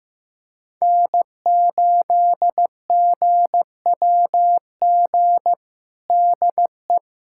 Morse code, heard as a keyed tone, about 15 wpm, 700 Hz. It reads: N8GWG DE
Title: USA Callsign DE Callsign - 5 Characters - 15wpm